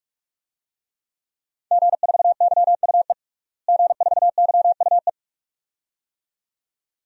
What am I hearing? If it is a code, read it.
Morse code, heard as a keyed tone, about 46 wpm, 700 Hz. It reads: G4YUE G4YUE
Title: DXCC-100 Call Signs - 5 Characters - Repeated 2x - 45wpm